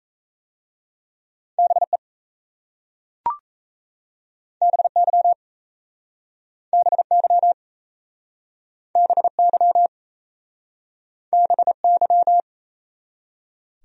Morse code, heard as a keyed tone, about 37 wpm, 700 Hz. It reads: BE BY BY BY BY